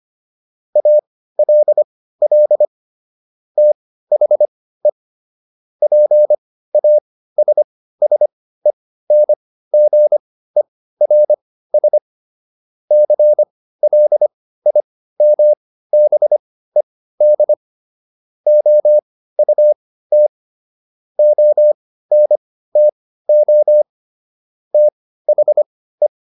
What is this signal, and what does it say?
Morse code, heard as a keyed tone, about 25 wpm, spaced as for 15 wpm, 600 Hz. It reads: ALL THE PASSENGERS CLIMBED OUT ONTO THE